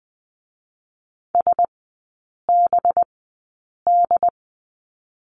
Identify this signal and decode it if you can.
Morse code, heard as a keyed tone, about 20 wpm, spaced as for 8 wpm, 700 Hz. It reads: SBD
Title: 184 Lesson 30e - Letter b - Three Characters 20wpm